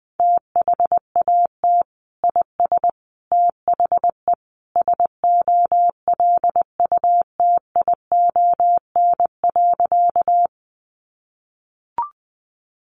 Morse code, heard as a keyed tone, about 20 wpm, 700 Hz. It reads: THAT IS THE SOLUTION.
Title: Sentences from Top 1300 Words 20wpm